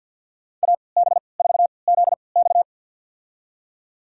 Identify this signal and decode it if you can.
Morse code, heard as a keyed tone, about 46 wpm, spaced as for 27 wpm, 700 Hz. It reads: AB4ZX
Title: USA Call Signs - 5 Characters - 3x Character Spacing - 50wpm